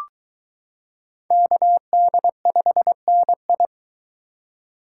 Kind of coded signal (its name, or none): Morse code